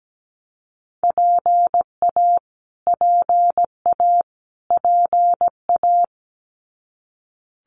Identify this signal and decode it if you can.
Morse code, heard as a keyed tone, about 17 wpm, 700 Hz. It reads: PA PA PA
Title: US States Repeated 3x 17wpm